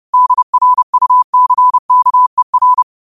Morse code, heard as a keyed tone, about 30 wpm, 1000 Hz. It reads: DRUCKER
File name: qotdfiles / 30wpm